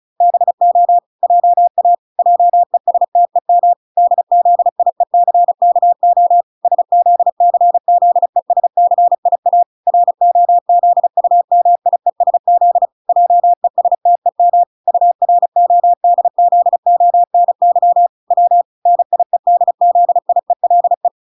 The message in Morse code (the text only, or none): BO JA JESTEM DZIECKO SZCZESCIA ROZUMIESZ JESTEM URODZONY W NIEDZIELE